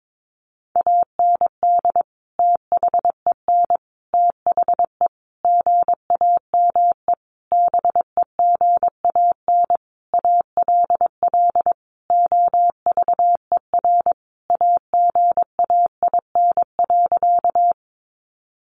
Morse code, heard as a keyed tone, about 22 wpm, 700 Hz. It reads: AND THEN THE GAME BEGAN ALL OVER AGAIN.
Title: Sentences from Top 500 Words 22wpm